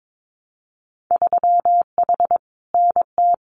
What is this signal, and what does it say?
Morse code, heard as a keyed tone, about 22 wpm, 700 Hz. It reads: 3H NT
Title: Winter Field Day Exchanges Encore 22wpm